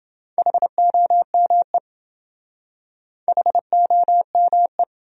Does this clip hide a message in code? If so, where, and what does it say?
Yes: Morse code from 0.4 to 4.8 s, heard as a keyed tone, about 27 wpm, 700 Hz. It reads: HOME HOME